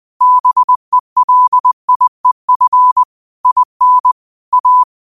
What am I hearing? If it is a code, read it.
Morse code, heard as a keyed tone, about 20 wpm, 1000 Hz. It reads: BELIEF IN A